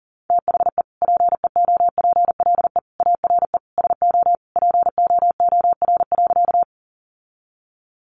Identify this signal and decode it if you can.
Morse code, heard as a keyed tone, about 40 wpm, 700 Hz. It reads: THE PEOPLE ARE SO POOR.
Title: Sentences from Top 1000 Words 40wpm